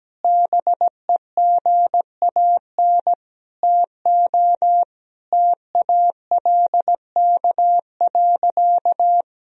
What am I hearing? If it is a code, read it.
Morse code, heard as a keyed tone, about 17 wpm, 700 Hz. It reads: BEGAN TO TALK.
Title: Sentences from Top 400 Words and Abbreviations 17wpm